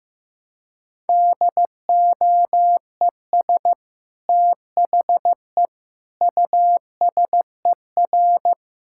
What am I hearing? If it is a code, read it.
Morse code, heard as a keyed tone, about 15 wpm, 700 Hz. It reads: DOES THE USER
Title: Sentences from Top 1000 Words - Review Part 5 - 15wpm